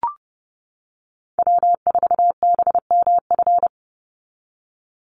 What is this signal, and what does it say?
Morse code, heard as a keyed tone, about 30 wpm, 700 Hz. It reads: W4BMF
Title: USA Call Signs 30wpm